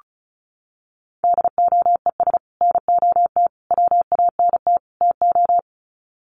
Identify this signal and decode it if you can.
Morse code, heard as a keyed tone, about 34 wpm, 700 Hz. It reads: DOES NOT WANT TO
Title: Sets of 4 Words 35wpm